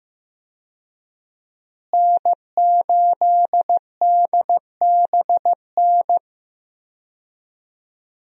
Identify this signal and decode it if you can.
Morse code, heard as a keyed tone, about 15 wpm, 700 Hz. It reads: N8DBN
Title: USA Call Signs - 5 Characters - Repeated 2x - 15wpm